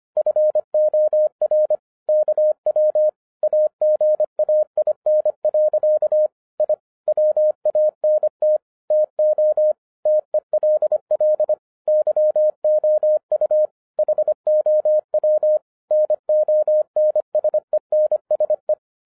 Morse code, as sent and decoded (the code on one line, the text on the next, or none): ..-. --- .-. / -.- .-- / .- --. .- .. -. .-.-.- / .. / .-- .- -. - / - --- / - . .-.. .-.. / -.-- --- ..- / .... --- .-- / -. --- -. ... -. ... .
FOR KW AGAIN. I WANT TO TELL YOU HOW NONSENSE